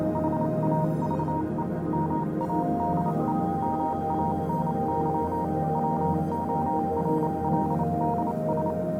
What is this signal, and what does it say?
Morse code, heard as a keyed tone, about 28 wpm, 940 Hz, with noise. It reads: HAVE MADE MANY WORLDS